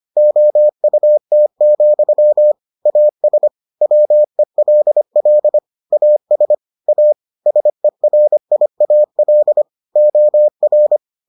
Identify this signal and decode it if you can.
Morse code, heard as a keyed tone, about 25 wpm, 600 Hz. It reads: OUT, AS WELL AS A SERIAL OR